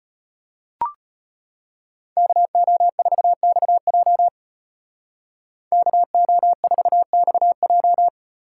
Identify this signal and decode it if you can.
Morse code, heard as a keyed tone, about 34 wpm, 700 Hz. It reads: KO4XJ KO4XJ